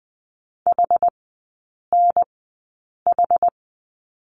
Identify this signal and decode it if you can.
Morse code, heard as a keyed tone, about 20 wpm, spaced as for 8 wpm, 700 Hz. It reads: HNH
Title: 107 Lesson 18e - Letter m - Three Characters 20wpm